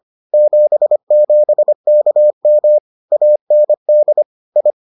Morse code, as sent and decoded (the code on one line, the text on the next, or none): --... --... -.- -- / .- -. -.. / ..
77KM AND I